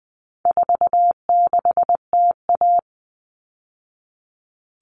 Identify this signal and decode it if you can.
Morse code, heard as a keyed tone, about 20 wpm, 700 Hz. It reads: 46TA
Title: All Cut Numbers - 4-Digit 20wpm